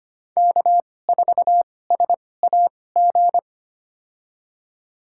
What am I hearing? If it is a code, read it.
Morse code, heard as a keyed tone, about 25 wpm, spaced as for 18 wpm, 700 Hz. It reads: K4SAG
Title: USA Call Signs - 5 Characters - 2x Character Spacing - 25wpm